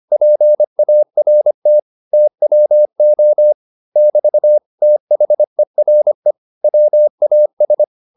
Morse code, heard as a keyed tone, about 25 wpm, 600 Hz. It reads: PART TWO <BT> THERE WAS